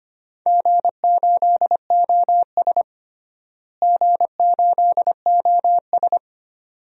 Morse code, heard as a keyed tone, about 25 wpm, 700 Hz. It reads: G8OS G8OS
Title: DXCC-100 Call Signs - 4 Characters - Repeated 2x - 25wpm